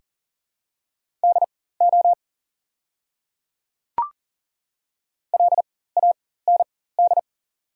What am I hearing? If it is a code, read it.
Morse code, heard as a keyed tone, about 40 wpm, spaced as for 19 wpm, 700 Hz. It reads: DO LAND
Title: 4x Character Spacing - Top 200 Words - 40wpm